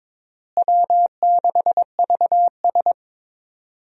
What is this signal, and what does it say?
Morse code, heard as a keyed tone, about 22 wpm, 700 Hz. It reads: W6VS
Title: USA Call Signs - 4 Characters - Repeated 3x - 22wpm